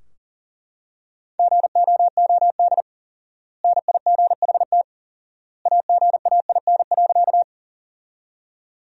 Morse code, heard as a keyed tone, about 40 wpm, 700 Hz. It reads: GOOD NIGHT AGAIN.
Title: 4x Word-Spacing - Sentences from Top 300 Words - 40wpm